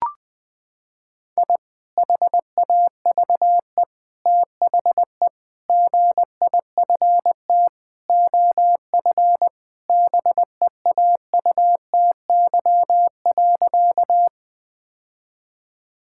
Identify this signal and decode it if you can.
Morse code, heard as a keyed tone, about 21 wpm, 700 Hz. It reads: I HAVE THE GIFT OF BEAUTY.
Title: Sentences from Top 1000 Words - Review Part 6 - 20wpm